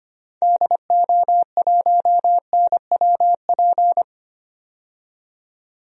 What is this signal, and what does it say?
Morse code, heard as a keyed tone, about 25 wpm, 700 Hz. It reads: DO1NWP
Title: DXCC-100 Call Signs - 6 Characters - 25wpm